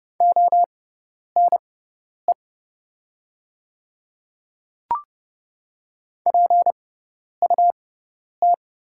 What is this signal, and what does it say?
Morse code, heard as a keyed tone, about 30 wpm, spaced as for 10 wpm, 700 Hz. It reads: ONE PUT
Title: 6x Character Spacing - Top 200 Words - 30wpm